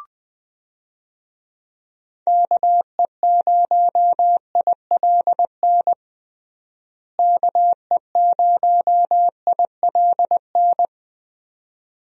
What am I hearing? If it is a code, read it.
Morse code, heard as a keyed tone, about 20 wpm, 700 Hz. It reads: KE0ILN KE0ILN
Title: USA Call Signs - 6 Characters - Repeated 2x - 20wpm